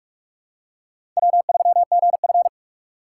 Morse code, heard as a keyed tone, about 46 wpm, 700 Hz. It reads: W3GF
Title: USA Call Signs - 4 Characters - 2x Character Spacing - 45wpm